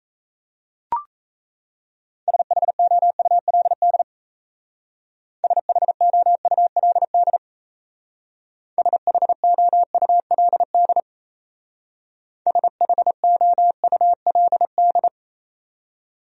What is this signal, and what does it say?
Morse code, heard as a keyed tone, about 35 wpm, 700 Hz. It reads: SHOULD SHOULD SHOULD SHOULD
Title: Speed-Racing Top 300 Words 28wpm